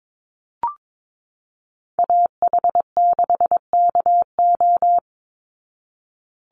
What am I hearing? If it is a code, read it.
Morse code, heard as a keyed tone, about 22 wpm, 700 Hz. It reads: AH6KO
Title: Call Signs Repeated 3x - 22wpm